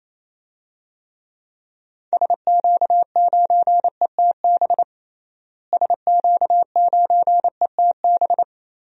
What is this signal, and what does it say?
Morse code, heard as a keyed tone, about 28 wpm, 700 Hz. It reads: SQ9ETB SQ9ETB
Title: DXCC-100 Call Signs - 6 Characters - Repeated 2x - 28wpm